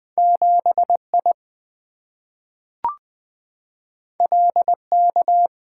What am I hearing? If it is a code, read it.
Morse code, heard as a keyed tone, about 20 wpm, 700 Hz. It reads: 7I LK
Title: Two Letter-Number Combinations - 20wpm